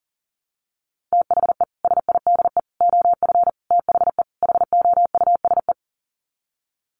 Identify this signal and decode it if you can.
Morse code, heard as a keyed tone, about 40 wpm, 700 Hz. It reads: THE SIDE OF THE HOUSE
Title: Sets of 5 Words Encore2 40wpm